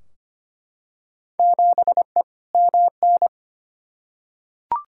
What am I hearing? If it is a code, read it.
Morse code, heard as a keyed tone, about 25 wpm, 700 Hz. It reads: 7E MN
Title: Field Day Exchanges Encore - 25wpm